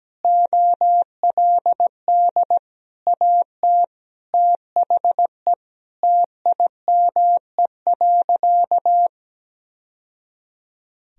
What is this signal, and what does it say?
Morse code, heard as a keyed tone, about 17 wpm, 700 Hz. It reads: OLD AT THE TIME.
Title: Sentences from Top 1000 Words - Review Part 2 - 17wpm